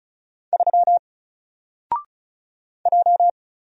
Morse code, heard as a keyed tone, about 36 wpm, 700 Hz. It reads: 3 J
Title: Single Letter-Number Alt 35wpm